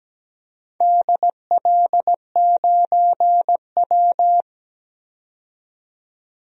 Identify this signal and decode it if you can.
Morse code, heard as a keyed tone, about 17 wpm, 700 Hz. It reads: DL9W
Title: DXCC-100 Call Signs - 4 Characters - 17wpm